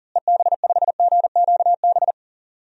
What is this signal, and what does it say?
Morse code, heard as a keyed tone, about 40 wpm, 700 Hz. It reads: EB5GQB
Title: DXCC-100 Call Signs - 6 Characters - Repeated 3x - 40wpm